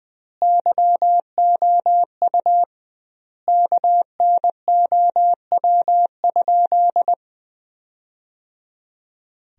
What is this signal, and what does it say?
Morse code, heard as a keyed tone, about 21 wpm, 700 Hz. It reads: YOU KNOW?
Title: 2x Word-Spacing - Sentences from Top 200 Words - 20wpm